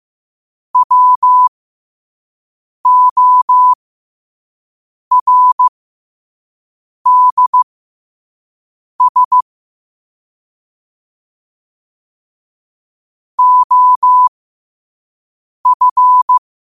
Morse code, heard as a keyed tone, about 15 wpm, spaced as for 5 wpm, 1000 Hz. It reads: WORDS OF
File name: qotdfiles / 5wpm